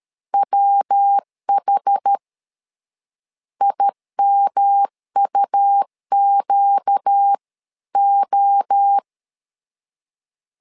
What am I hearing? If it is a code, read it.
Morse code, heard as a keyed tone, about 13 wpm, 800 Hz. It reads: WH IMUQO
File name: LEZIONI / 4 Q1S3 12wpm 2023